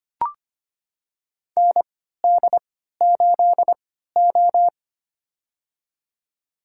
Morse code, heard as a keyed tone, about 25 wpm, spaced as for 14 wpm, 700 Hz. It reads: ND8O
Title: USA Call Signs - 4 Characters - 3x Character Spacing - 25wpm